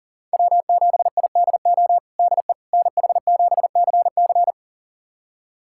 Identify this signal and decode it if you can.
Morse code, heard as a keyed tone, about 40 wpm, 700 Hz. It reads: W7IDO DE NH7CC
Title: USA Callsign DE Callsign - 5 Characters - 40wpm